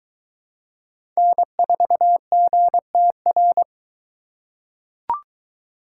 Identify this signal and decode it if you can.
Morse code, heard as a keyed tone, about 20 wpm, 700 Hz. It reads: N4GTR